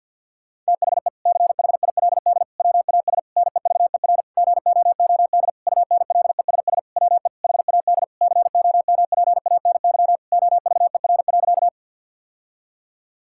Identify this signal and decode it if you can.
Morse code, heard as a keyed tone, about 51 wpm, 700 Hz. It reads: THE CHILD WAS NEVER GOOD UNLESS WE HAD COMPANY OVER.
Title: Sentences from Top 1000 Words - Review Part 6 - 50wpm